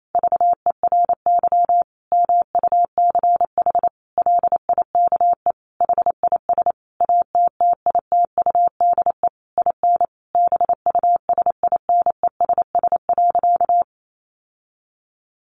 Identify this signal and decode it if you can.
Morse code, heard as a keyed tone, about 28 wpm, 700 Hz. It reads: VERY MUCH LIKE HIS ATTITUDE IN BUSINESS.
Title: Sentences from Top 1300 Words 28wpm